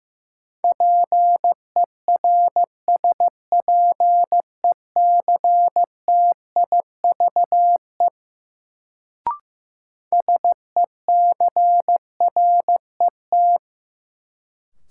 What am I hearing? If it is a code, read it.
Morse code, heard as a keyed tone, about 15 wpm, 700 Hz. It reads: PERSPECTIVE SECRET
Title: Sentences from Top 1200 Words - New Word Drill - 15wpm